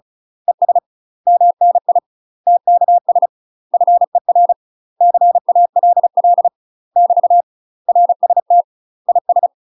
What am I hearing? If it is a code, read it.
Morse code, heard as a keyed tone, about 36 wpm, 700 Hz. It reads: ES MNI TKS FER CALL <BT> RST IS
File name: UK MASTER QSOs TRAINING / 35WPM / UK MASTER QSOs 0002